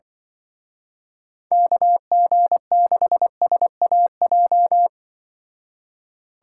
Morse code, heard as a keyed tone, about 23 wpm, 700 Hz. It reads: KG6SAJ
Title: Speed Racing - USA 2x3 Call Signs 20wpm